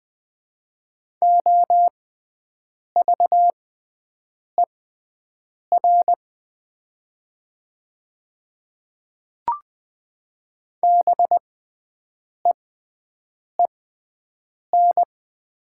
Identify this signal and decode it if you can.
Morse code, heard as a keyed tone, about 20 wpm, spaced as for 7 wpm, 700 Hz. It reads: OVER BEEN
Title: 6x Character Spacing - Top 100 Words - 20wpm